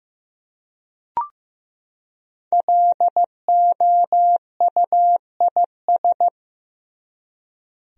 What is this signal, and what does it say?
Morse code, heard as a keyed tone, about 15 wpm, 700 Hz. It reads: LOUIS
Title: USA Common Names - Max 5 Letters - 15wpm